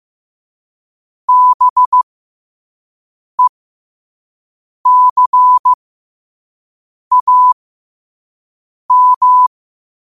Morse code, heard as a keyed tone, about 15 wpm, 1000 Hz. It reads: B E C A M